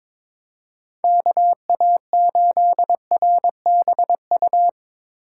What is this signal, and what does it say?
Morse code, heard as a keyed tone, about 24 wpm, 700 Hz. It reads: KA8RBU
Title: Speed Racing - USA 2x3 Call Signs 22wpm